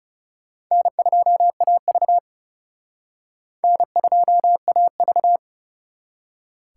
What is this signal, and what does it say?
Morse code, heard as a keyed tone, about 36 wpm, 700 Hz. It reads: N2AV N2AV
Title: Speed Racing - USA 1x2 Call Signs 30wpm